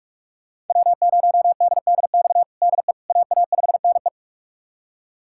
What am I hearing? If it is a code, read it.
Morse code, heard as a keyed tone, about 43 wpm, 700 Hz. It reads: W0DDX DE AA5NE
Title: USA Callsign DE Callsign - 5 Characters - 45wpm